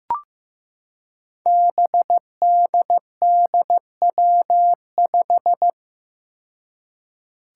Morse code, heard as a keyed tone, about 16 wpm, 700 Hz. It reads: BDDW5
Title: Five Challenging Letter-Number Combinations 15wpm